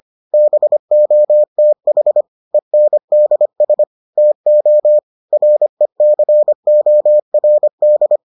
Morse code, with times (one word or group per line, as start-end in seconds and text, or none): 0.3-2.2 BOTH
2.5-3.8 ENDS
4.2-5.0 TO
5.3-8.2 RECORD